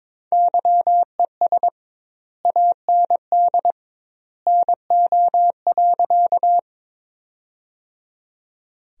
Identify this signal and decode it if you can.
Morse code, heard as a keyed tone, about 22 wpm, 700 Hz. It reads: YES AND NO.
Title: 2x Word-Spacing - Sentences from Top 500 Words - 22wpm